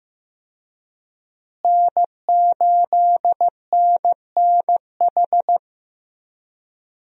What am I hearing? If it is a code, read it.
Morse code, heard as a keyed tone, about 16 wpm, 700 Hz. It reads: N8NNH
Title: USA Call Signs - 5 Characters - Repeated 2x - 15wpm